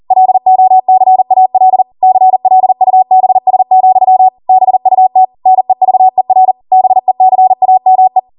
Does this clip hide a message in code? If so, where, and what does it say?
Yes: Morse code from 0.1 to 8.2 s, heard as a keyed tone, about 40 wpm, 750 Hz. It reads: LOCAL CLUBS, BUT NEVER BECAME